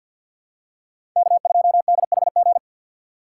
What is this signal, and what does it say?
Morse code, heard as a keyed tone, about 47 wpm, 700 Hz. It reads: K2DHG
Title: USA Call Signs - 5 Characters - 4x Character Spacing - 50wpm